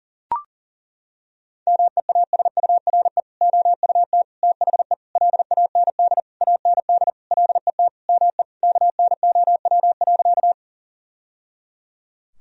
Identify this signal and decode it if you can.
Morse code, heard as a keyed tone, about 37 wpm, 700 Hz. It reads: MEASURE OUT THE LAND AND LET ME KNOW.
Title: Sentences from Top 1000 Words - Review Part 2 - 40wpm